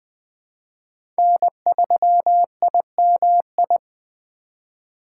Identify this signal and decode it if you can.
Morse code, heard as a keyed tone, about 20 wpm, 700 Hz. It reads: N3IMI
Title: USA Call Signs - 5 Characters - 3x Character Spacing - 20wpm